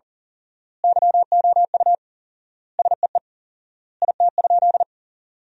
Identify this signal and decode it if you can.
Morse code, heard as a keyed tone, about 37 wpm, 700 Hz. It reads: YOU SEE IT?
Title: 4x Word-Spacing - Sentences from Top 100 Words - 40wpm